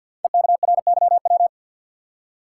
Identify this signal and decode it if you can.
Morse code, heard as a keyed tone, about 48 wpm, 700 Hz. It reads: EKRYW
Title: Code Groups - 5 Letters - 50wpm